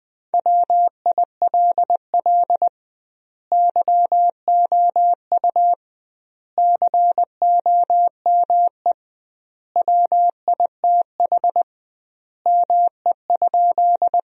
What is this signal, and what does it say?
Morse code, heard as a keyed tone, about 20 wpm, 700 Hz. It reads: WILL YOU COME WITH ME?
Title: 2x Word-Spacing - Sentences from Top 200 Words - 20wpm